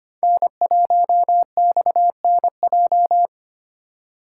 Morse code, heard as a keyed tone, about 25 wpm, 700 Hz. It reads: N1XNJ
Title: USA Call Signs - 5 Characters - 2x Character Spacing - 25wpm